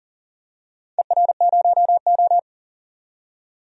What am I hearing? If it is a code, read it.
Morse code, heard as a keyed tone, about 40 wpm, 700 Hz. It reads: ER0O
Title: DXCC-100 Call Signs - 4 Characters - 40wpm